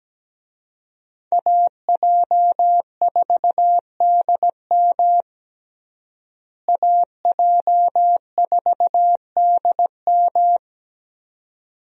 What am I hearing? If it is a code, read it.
Morse code, heard as a keyed tone, about 17 wpm, 700 Hz. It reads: AJ4DM AJ4DM
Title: USA Call Signs - 5 Characters - Repeated 2x - 17wpm